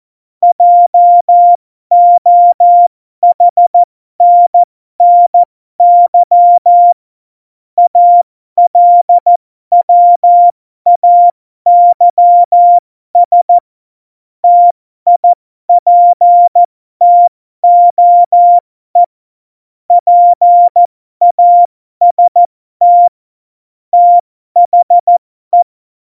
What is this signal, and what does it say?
Morse code, heard as a keyed tone, about 14 wpm, spaced as for 12 wpm, 700 Hz. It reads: JOHNNY ALWAYS TIPTOE PAST THE
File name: Morsekurs / Witze